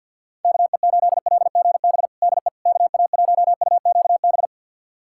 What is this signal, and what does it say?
Morse code, heard as a keyed tone, about 52 wpm, 700 Hz. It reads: KE8LGB DE KA1UQB
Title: USA Callsign DE Callsign - 6 Characters - 50wpm